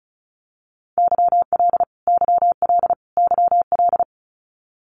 Morse code, heard as a keyed tone, about 34 wpm, 700 Hz. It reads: YL YL YL
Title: Abbreviations Repeated 3x - 35wpm